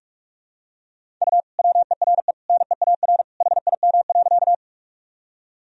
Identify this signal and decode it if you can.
Morse code, heard as a keyed tone, about 41 wpm, 700 Hz. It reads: U WERE NEAR HIM.
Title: Sentences from Top 200 Words and Abbreviations 45wpm